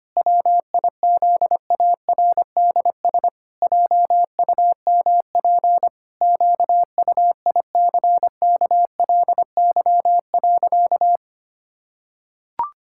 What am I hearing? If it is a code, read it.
Morse code, heard as a keyed tone, about 25 wpm, 700 Hz. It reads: WIZARDS JUMP QUICKLY.
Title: Pangram Challenge v2 - 25wpm